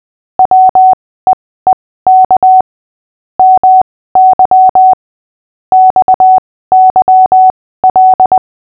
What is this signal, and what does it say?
Morse code, heard as a keyed tone, about 20 wpm, spaced as for 15 wpm, 750 Hz. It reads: WEEK MY XYL
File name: cw / fists / Short-Story-5-15wpm